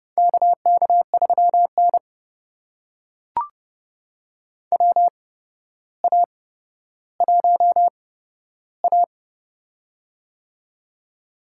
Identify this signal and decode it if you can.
Morse code, heard as a keyed tone, about 30 wpm, spaced as for 13 wpm, 700 Hz. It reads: KK3N WA1A
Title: USA Call Signs - 4 Characters - 8x Character Spacing - 30wpm